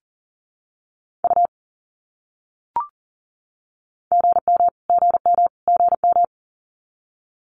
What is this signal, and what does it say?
Morse code, heard as a keyed tone, about 40 wpm, 700 Hz. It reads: U GM GM GM
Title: Abbreviations Repeated 3x - 40wpm